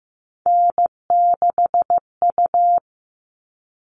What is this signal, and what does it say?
Morse code, heard as a keyed tone, about 15 wpm, 700 Hz. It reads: N6U
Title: All Cut Numbers - 3-Digit 15wpm